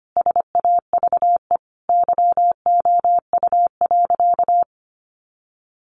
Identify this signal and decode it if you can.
Morse code, heard as a keyed tone, about 25 wpm, 700 Hz. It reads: SAVE YOU.
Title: Sentences from Top 1100 Words 25wpm